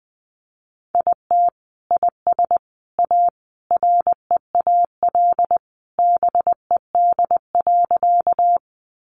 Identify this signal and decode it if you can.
Morse code, heard as a keyed tone, about 20 wpm, 700 Hz. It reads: IT IS A REAL BED.
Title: Sentences from Top 500 Words 20wpm